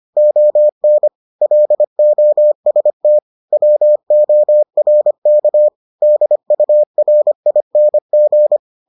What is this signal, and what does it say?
Morse code, heard as a keyed tone, about 25 wpm, 600 Hz. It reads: ON LOST WORK DURING